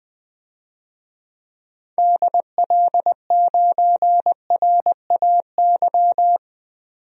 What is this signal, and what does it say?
Morse code, heard as a keyed tone, about 20 wpm, 700 Hz. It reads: DL9RAY
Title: DXCC-100 Call Signs - 6 Characters - 20wpm